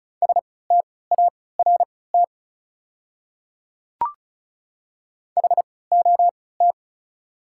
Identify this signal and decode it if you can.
Morse code, heard as a keyed tone, about 34 wpm, spaced as for 20 wpm, 690 Hz. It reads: START HOT
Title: 3x Character Spacing - Top 500 Words - 35wpm